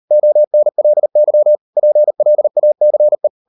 Morse code, heard as a keyed tone, about 38 wpm, 600 Hz. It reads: ONLY PLACE